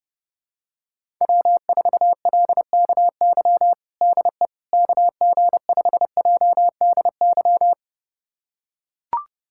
Morse code, heard as a keyed tone, about 30 wpm, 700 Hz. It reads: W4LKY DE KG5JDY
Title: USA Callsign DE Callsign 30wpm